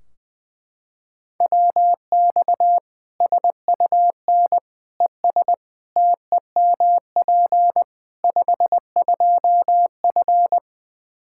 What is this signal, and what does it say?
Morse code, heard as a keyed tone, about 20 wpm, 700 Hz. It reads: WX SUN ES TEMP 52F
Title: Practice QSO - Exchange 2 - Info - 20wpm